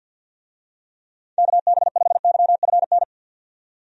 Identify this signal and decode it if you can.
Morse code, heard as a keyed tone, about 51 wpm, 700 Hz. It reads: KB5YFN
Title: USA Call Signs - 6 Characters - 50wpm